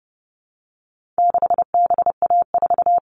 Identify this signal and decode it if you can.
Morse code, heard as a keyed tone, about 30 wpm, 700 Hz. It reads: 6BA4
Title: All Cut Numbers - 4-Digit 30wpm